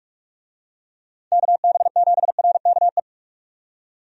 Morse code, heard as a keyed tone, about 46 wpm, 700 Hz. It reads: KB7RKE
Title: USA Call Signs - 6 Characters - 45wpm